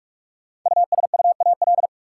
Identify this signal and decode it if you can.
Morse code, heard as a keyed tone, about 46 wpm, 700 Hz. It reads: USUAL
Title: Sentences from Top 400 Words - New Word Drill - 45wpm